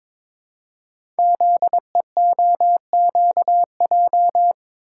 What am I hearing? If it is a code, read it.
Morse code, heard as a keyed tone, about 23 wpm, 700 Hz. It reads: ZEOQJ
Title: Code Groups - 5 Letters - 22wpm